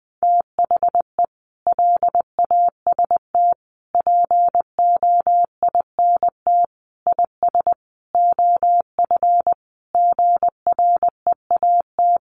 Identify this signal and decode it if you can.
Morse code, heard as a keyed tone, about 20 wpm, 700 Hz. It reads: THE LAST POINT IS OF GREAT
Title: Sentences from Top 1200 Words 20wpm